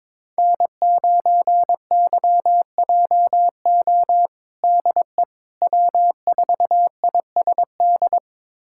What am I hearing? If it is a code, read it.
Morse code, heard as a keyed tone, about 22 wpm, 700 Hz. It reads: N9YJO DE W4ISD
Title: USA Callsign DE Callsign - 5 Characters - 22wpm